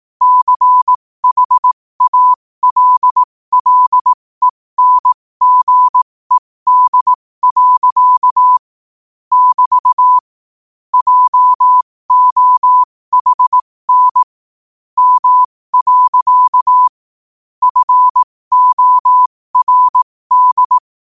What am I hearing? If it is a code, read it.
Morse code, heard as a keyed tone, about 18 wpm, spaced as for 15 wpm, 1000 Hz. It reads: CHALLENGED. <BT> JOHN M. FORD